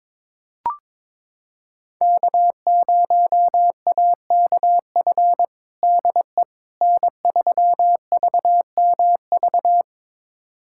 Morse code, heard as a keyed tone, about 22 wpm, 700 Hz. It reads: K0AKF DE N3VMV
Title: USA Callsign DE Callsign - 5 Characters - 22wpm